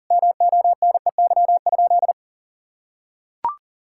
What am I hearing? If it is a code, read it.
Morse code, heard as a keyed tone, about 40 wpm, 700 Hz. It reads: MONEY?
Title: Sentences from Top 1000 Words - Review Part 5 - 40wpm